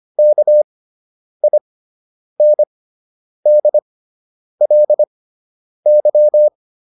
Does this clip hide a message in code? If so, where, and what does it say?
Yes: Morse code from 0.2 to 6.5 s, heard as a keyed tone, about 25 wpm, spaced as for 9 wpm, 600 Hz. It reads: KINDLY